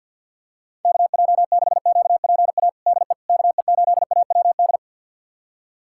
Morse code, heard as a keyed tone, about 52 wpm, 690 Hz. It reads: KJ6QPA DE KE8AWD
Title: USA Callsign DE Callsign - 6 Characters - 50wpm